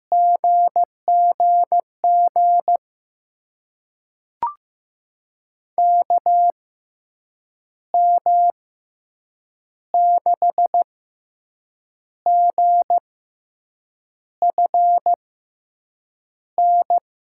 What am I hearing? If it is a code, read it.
Morse code, heard as a keyed tone, about 16 wpm, spaced as for 6 wpm, 700 Hz. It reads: GGG KM6GFN
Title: USA Call Signs - 6 Characters - 6x Character Spacing - 15wpm